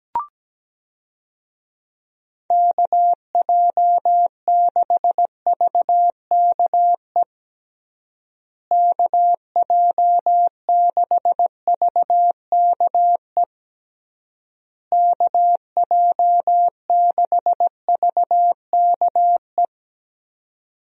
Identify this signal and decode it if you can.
Morse code, heard as a keyed tone, about 17 wpm, 700 Hz. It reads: KJ6VKE KJ6VKE KJ6VKE
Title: USA Call Signs - 6 Characters - Repeated 3x - 17wpm